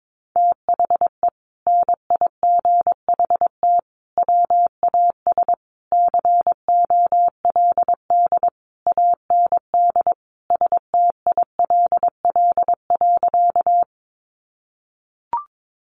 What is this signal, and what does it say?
Morse code, heard as a keyed tone, about 22 wpm, 700 Hz. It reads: THE NIGHT WAS COLD AND STILL.
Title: Sentences from Top 400 Words 22wpm